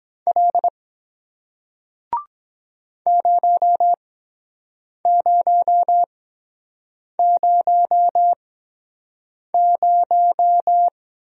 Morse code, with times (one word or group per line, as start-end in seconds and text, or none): 0.3-0.7 L
3.1-10.9 0000